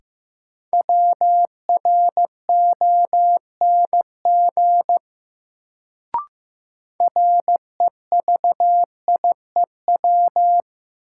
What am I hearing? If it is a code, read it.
Morse code, heard as a keyed tone, about 15 wpm, 700 Hz. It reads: WRONG REVIEW
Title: Sentences from Top 900 Words - New Word Drill - 15wpm